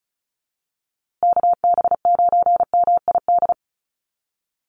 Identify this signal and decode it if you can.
Morse code, heard as a keyed tone, about 34 wpm, 700 Hz. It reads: KB9MID
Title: USA Call Signs Repeated 2x 35wpm